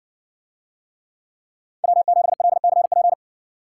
Morse code, heard as a keyed tone, about 60 wpm, 700 Hz. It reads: W7LCP